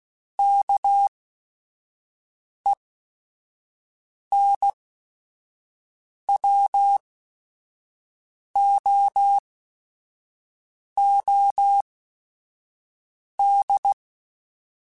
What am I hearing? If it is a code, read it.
Morse code, heard as a keyed tone, about 16 wpm, spaced as for 5 wpm, 800 Hz. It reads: KENWOOD